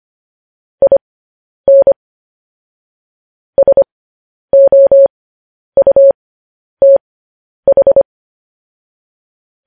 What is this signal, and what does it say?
Morse code, heard as a keyed tone, about 25 wpm, spaced as for 10 wpm, 550 Hz. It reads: IN SOUTH
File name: podcast / News headlines-1x-25x10